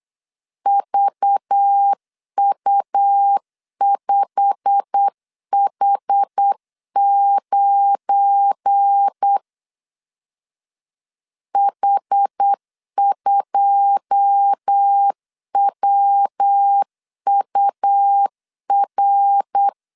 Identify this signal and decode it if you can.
Morse code, heard as a keyed tone, about 8 wpm, 800 Hz. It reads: VU5H9 H2WUR
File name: CW / DET / 1 9 1500C S C 8wpm 5653